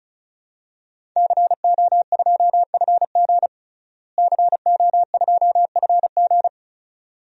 Morse code, heard as a keyed tone, about 35 wpm, 700 Hz. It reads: CO2FG CO2FG
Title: DXCC-100 Call Signs - 5 Characters - Repeated 2x - 35wpm